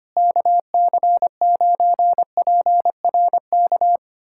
Morse code, heard as a keyed tone, about 25 wpm, 700 Hz. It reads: KC9PRK
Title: USA Callsign DE Callsign - 6 Characters - 25wpm